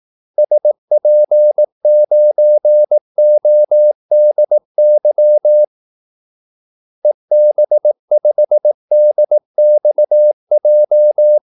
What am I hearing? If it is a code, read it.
Morse code, heard as a keyed tone, about 18 wpm, 600 Hz. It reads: SP9ODY EB5DXJ
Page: Kaldesignaler 18 wpm